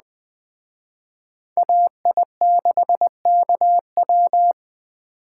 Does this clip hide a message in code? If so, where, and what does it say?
Yes: Morse code from 1.6 to 4.5 s, heard as a keyed tone, about 19 wpm, 700 Hz. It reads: AI6KW